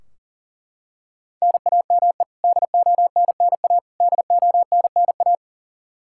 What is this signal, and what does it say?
Morse code, heard as a keyed tone, about 40 wpm, 700 Hz. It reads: NAME DONNA DONNA
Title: K1USN SST QSO Practice 40wpm